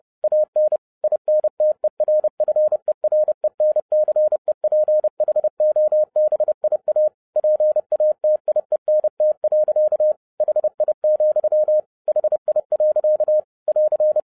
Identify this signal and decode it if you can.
Morse code, heard as a keyed tone, about 30 wpm, 610 Hz. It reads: AN INTERFERENCEPHOBIA PATIENT. HI, HI. <AR>
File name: 2011-07-mp3 / n40 EXCITING QRP CW OPERATION 30wpm